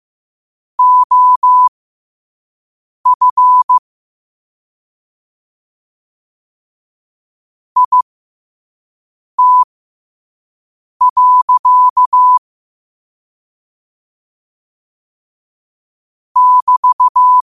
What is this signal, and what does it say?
Morse code, heard as a keyed tone, about 15 wpm, spaced as for 5 wpm, 1000 Hz. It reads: OF IT. <BT>
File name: qotdfiles / 5wpm